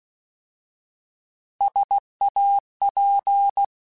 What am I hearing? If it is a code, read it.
Morse code, heard as a keyed tone, about 16 wpm, 800 Hz. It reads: SAP